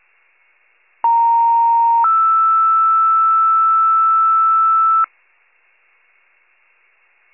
Two-tone page, heard as a one-second tone, then a three-second tone, about 930 then 1320 Hz. It is heard from 1.0 to 5.0 s.